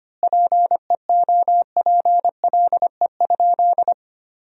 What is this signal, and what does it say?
Morse code, heard as a keyed tone, about 25 wpm, 700 Hz. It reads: PEOPLE?
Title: 4x Word-Spacing - Sentences from Top 200 Words - 25wpm